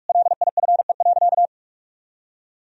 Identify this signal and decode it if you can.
Morse code, heard as a keyed tone, about 45 wpm, 700 Hz. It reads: LIFE.
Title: Sentences from Top 1000 Words - Review Part 4 - 45wpm